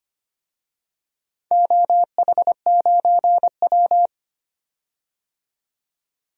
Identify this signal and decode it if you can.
Morse code, heard as a keyed tone, about 25 wpm, 700 Hz. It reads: OH9W
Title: DXCC-100 Call Signs - 4 Characters - 25wpm